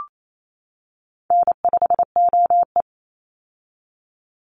Morse code, heard as a keyed tone, about 28 wpm, 700 Hz. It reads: N5OE
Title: Call Signs - 28wpm